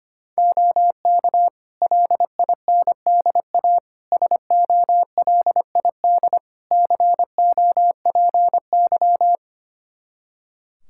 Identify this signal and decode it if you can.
Morse code, heard as a keyed tone, about 25 wpm, 700 Hz. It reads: OK LINDA SOLID COPY
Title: Practice QSO - Exchange 3 - More Info - 25wpm